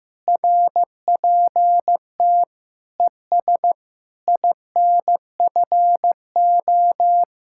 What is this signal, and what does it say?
Morse code, heard as a keyed tone, about 15 wpm, 700 Hz. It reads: RPT ES INFO
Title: Speed-Racing Common QSO Phrases Encore - 15wpm